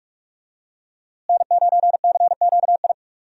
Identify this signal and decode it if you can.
Morse code, heard as a keyed tone, about 42 wpm, 700 Hz. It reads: N9CQI
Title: USA Call Signs - 5 Characters - Repeated 2x - 45wpm